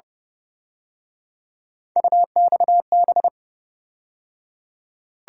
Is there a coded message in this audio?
Yes: Morse code from 2.0 to 3.3 s, heard as a keyed tone, about 29 wpm, 700 Hz. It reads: UXB